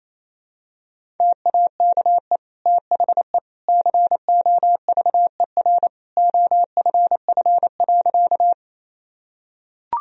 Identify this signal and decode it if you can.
Morse code, heard as a keyed tone, about 28 wpm, 700 Hz. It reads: TAKE THE COVER OFF.
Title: Sentences from Top 1000 Words - Review Part 1 - 28wpm